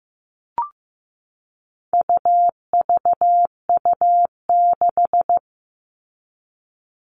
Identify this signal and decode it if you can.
Morse code, heard as a keyed tone, about 15 wpm, 700 Hz. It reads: UVU6
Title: All Cut Numbers - 4-Digit 15wpm